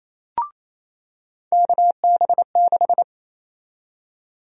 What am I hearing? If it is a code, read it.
Morse code, heard as a keyed tone, about 28 wpm, 700 Hz. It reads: KB6
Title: USA Call Sign Prefixes - 28wpm